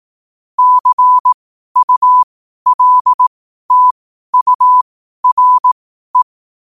Morse code, heard as a keyed tone, about 18 wpm, spaced as for 12 wpm, 1000 Hz. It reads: CULTURE
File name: qotdfiles / 12wpm